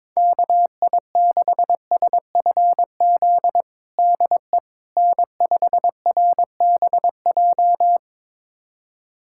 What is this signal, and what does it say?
Morse code, heard as a keyed tone, about 22 wpm, 700 Hz. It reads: KI6SFZ DE N5RBJ
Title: USA Callsign DE Callsign 22wpm